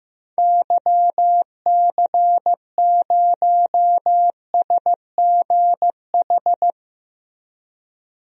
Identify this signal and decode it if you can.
Morse code, heard as a keyed tone, about 15 wpm, 700 Hz. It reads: YC0SGH
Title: DXCC-100 Call Signs - 6 Characters - Repeated 3x - 15wpm